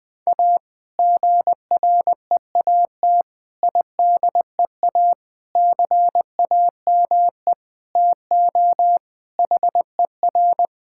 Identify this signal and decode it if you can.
Morse code, heard as a keyed tone, about 20 wpm, 700 Hz. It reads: A GREAT IDEA CAME TO HER
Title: Sentences with 2-Word Phrases - Spoken Beginning - 20wpm